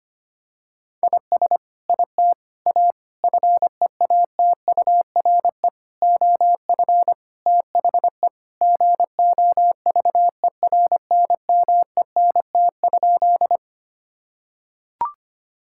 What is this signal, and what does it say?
Morse code, heard as a keyed tone, about 25 wpm, 700 Hz. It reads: IS IT A FEATURE OF THE GOVERNMENT?
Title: Sentences from Top 1000 Words - Review Part 5 - 25wpm